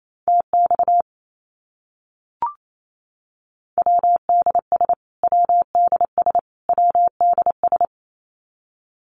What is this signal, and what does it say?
Morse code, heard as a keyed tone, about 28 wpm, 700 Hz. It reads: TX WDS WDS WDS
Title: Abbreviations Repeated 3x - 28wpm